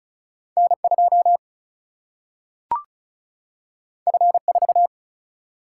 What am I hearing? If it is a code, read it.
Morse code, heard as a keyed tone, about 36 wpm, 700 Hz. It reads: N 2 F 4
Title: Single-Letter Single-Number 35wpm